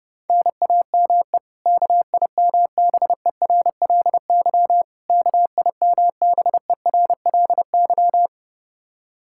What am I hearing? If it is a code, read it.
Morse code, heard as a keyed tone, about 30 wpm, 700 Hz. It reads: NAME KIMBERLY KIMBERLY
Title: Practice QSO - Exchange 1 - Beginning - 30wpm